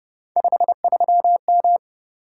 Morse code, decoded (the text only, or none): 53M